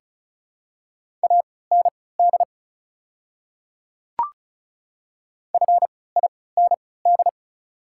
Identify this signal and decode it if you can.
Morse code, heard as a keyed tone, about 36 wpm, spaced as for 20 wpm, 700 Hz. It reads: AND FIND
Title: 3x Character Spacing - Top 200 Words - 35wpm